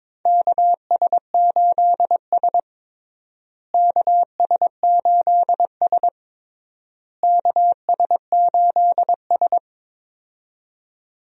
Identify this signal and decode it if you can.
Morse code, heard as a keyed tone, about 22 wpm, 700 Hz. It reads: KS8S KS8S KS8S
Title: USA Call Signs - 4 Characters - Repeated 3x - 22wpm